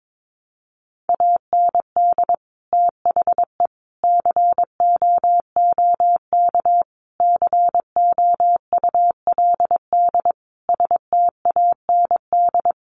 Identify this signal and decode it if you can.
Morse code, heard as a keyed tone, about 22 wpm, 700 Hz. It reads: AND THE COOK COULD STAND
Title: Sentences from Top 1300 Words 22wpm